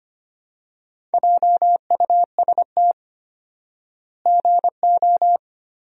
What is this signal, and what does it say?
Morse code, heard as a keyed tone, about 25 wpm, 700 Hz. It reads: JUST GO
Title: 4x Word-Spacing - Sentences from Top 500 Words - 25wpm